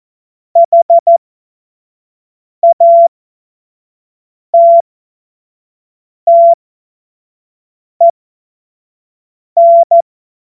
Morse code, heard as a keyed tone, about 14 wpm, spaced as for 5 wpm, 680 Hz. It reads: HATTEN
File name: Morsekurs / Witze / Witz 06. September 2022 25 F70